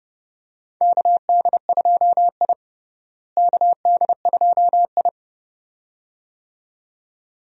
Morse code, heard as a keyed tone, about 32 wpm, 700 Hz. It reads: KD2I KD2I
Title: USA Call Signs - 4 Characters - Repeated 2x - 30wpm